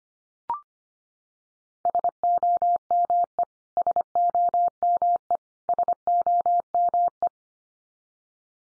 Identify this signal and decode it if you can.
Morse code, heard as a keyed tone, about 25 wpm, 700 Hz. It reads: SOME SOME SOME
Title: Top 100 Words 25wpm x3